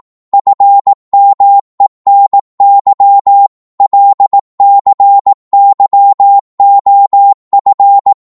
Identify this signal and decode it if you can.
Morse code, heard as a keyed tone, about 18 wpm, 800 Hz. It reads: FMENY LCYOF
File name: Morse / ltr-18wpm